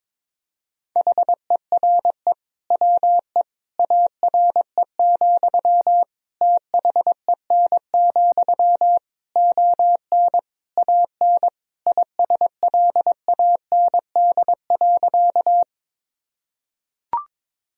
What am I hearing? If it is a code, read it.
Morse code, heard as a keyed tone, about 22 wpm, 700 Hz. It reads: HERE WE ARE, THEN, ON AN ISLAND.
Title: Sentences from Top 1000 Words - Review Part 3 - 22wpm